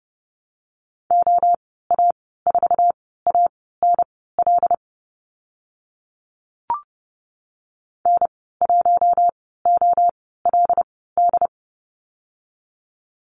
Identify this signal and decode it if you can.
Morse code, heard as a keyed tone, about 30 wpm, spaced as for 17 wpm, 700 Hz. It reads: OA4ANL N1OLD
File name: Lesson 14g - Number 2 - Call Signs 30wpm